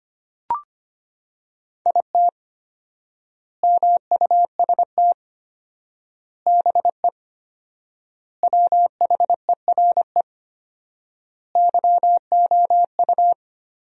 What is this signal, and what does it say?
Morse code, heard as a keyed tone, about 25 wpm, 700 Hz. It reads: IT MUST BE WHERE YOU
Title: 4x Word-Spacing - Sentences from Top 200 Words - 25wpm